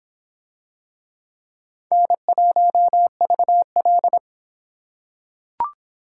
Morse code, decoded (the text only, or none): N1VL